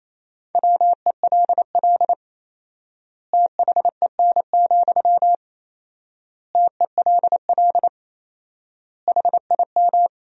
Morse code, heard as a keyed tone, about 28 wpm, 700 Hz. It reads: WELL THEN, TELL HIM
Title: 4x Word-Spacing - Sentences from Top 200 Words - 28wpm